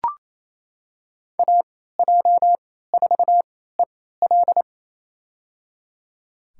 Morse code, heard as a keyed tone, about 27 wpm, spaced as for 16 wpm, 700 Hz. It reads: AJ4EL
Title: USA Call Signs - 3x Character Spacing - 28wpm